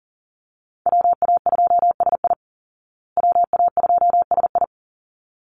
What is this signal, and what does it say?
Morse code, heard as a keyed tone, about 40 wpm, 700 Hz. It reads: WA2SI WA2SI
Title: Call Signs Repeated 3x - 40wpm